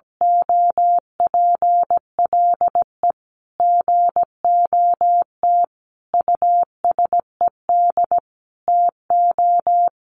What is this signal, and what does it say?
Morse code, heard as a keyed tone, about 17 wpm, 700 Hz. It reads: OPLE GOT USED TO
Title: Sentences from Top 700 Words 17wpm